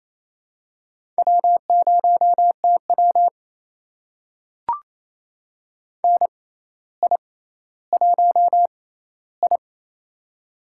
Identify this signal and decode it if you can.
Morse code, heard as a keyed tone, about 27 wpm, spaced as for 14 wpm, 700 Hz. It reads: W0TW NI1I
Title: USA Call Signs - 4 Characters - 6x Character Spacing - 28wpm